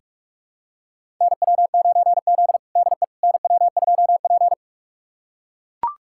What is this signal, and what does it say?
Morse code, heard as a keyed tone, about 44 wpm, 690 Hz. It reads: NW9Z DE NW2P
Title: USA Callsign DE Callsign - 4 Characters - 45wpm